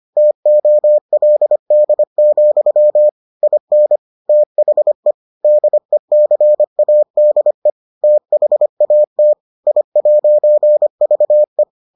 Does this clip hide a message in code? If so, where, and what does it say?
Yes: Morse code from 0.2 to 11.6 s, heard as a keyed tone, about 25 wpm, 600 Hz. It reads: TOLD, IN THE DECADE THAT I'VE